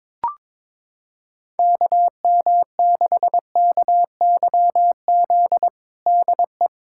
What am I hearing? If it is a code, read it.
Morse code, heard as a keyed tone, about 22 wpm, 700 Hz. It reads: KM6KYZ DE
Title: USA Callsign DE Callsign 22wpm